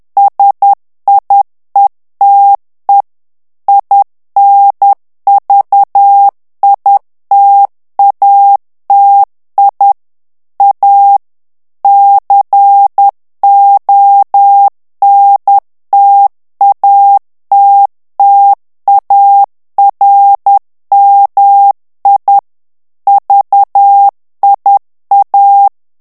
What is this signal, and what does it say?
Morse code, heard as a keyed tone, about 10 wpm, 800 Hz. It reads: SIETE INVITATI A CONTATTARMI VIA
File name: morse mp3 / ITALIEN texte clair 1 vitesse 10 mots minute